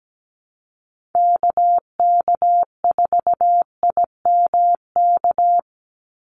Morse code, heard as a keyed tone, about 17 wpm, 700 Hz. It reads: KK4IMK
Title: USA Call Signs 17wpm